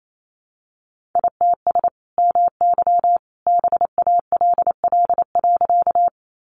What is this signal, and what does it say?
Morse code, heard as a keyed tone, about 28 wpm, 700 Hz. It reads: ITS MY BALL.
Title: Sentences from Top 900 Words 28wpm